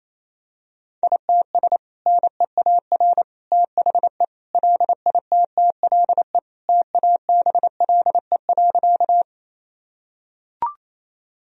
Morse code, heard as a keyed tone, about 28 wpm, 700 Hz. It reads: ITS NEAR THE LITTLE TABLE.
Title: Sentences from Top 1000 Words - Review Part 2 - 28wpm